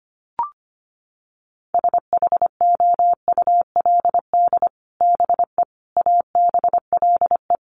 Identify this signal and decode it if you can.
Morse code, heard as a keyed tone, about 25 wpm, 700 Hz. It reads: SHOULD BE ABLE
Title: Sets of 4 Words 25wpm